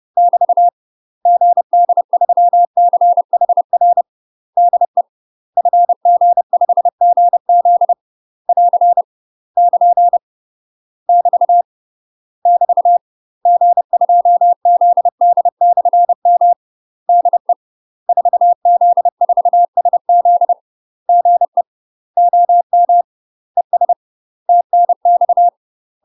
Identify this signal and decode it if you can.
Morse code, heard as a keyed tone, about 30 wpm, 700 Hz. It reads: <BT> GD3CHR DE FG5GZ <AR> <KN> <BT> <BT> G2ZD/M DE 4Z4SZ GE OM ES TNX
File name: UK MASTER QSOs TRAINING / 30WPM / UK MASTER QSOs 0024